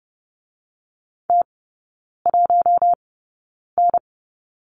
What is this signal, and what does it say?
Morse code, heard as a keyed tone, about 30 wpm, spaced as for 9 wpm, 700 Hz. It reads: T1N
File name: Lesson 16e - Letter c - Three Characters 30wpm